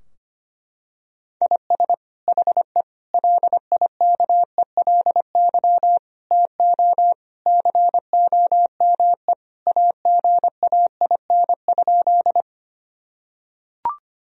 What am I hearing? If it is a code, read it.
Morse code, heard as a keyed tone, about 25 wpm, 700 Hz. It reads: IS HE LIKELY TO COME AGAIN?
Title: Sentences from Top 1000 Words - Review Part 4 - 25wpm